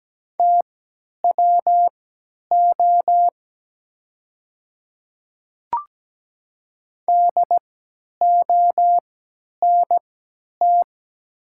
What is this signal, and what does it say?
Morse code, heard as a keyed tone, about 17 wpm, spaced as for 10 wpm, 690 Hz. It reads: TWO DONT
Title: 3x Character Spacing - Top 300 Words - 17wpm